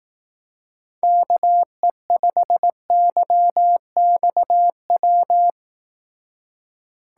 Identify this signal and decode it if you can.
Morse code, heard as a keyed tone, about 18 wpm, 700 Hz. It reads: KE5YXW
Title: Speed Racing - USA 2x3 Call Signs 15wpm